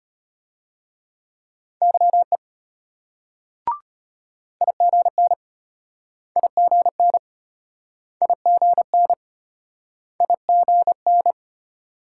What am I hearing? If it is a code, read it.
Morse code, heard as a keyed tone, about 34 wpm, 700 Hz. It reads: YE IGN IGN IGN IGN